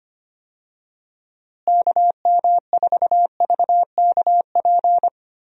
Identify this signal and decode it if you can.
Morse code, heard as a keyed tone, about 25 wpm, 700 Hz. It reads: KM4VKP
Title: USA Call Signs - 6 Characters - Repeated 2x - 25wpm